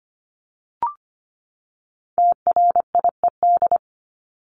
Morse code, heard as a keyed tone, about 25 wpm, 700 Hz. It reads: TRIED
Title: ED Suffix - Top 100 Words with Base Words 25wpm